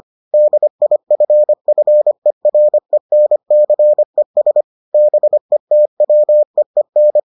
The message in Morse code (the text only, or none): DIFFERENCES BETWEEN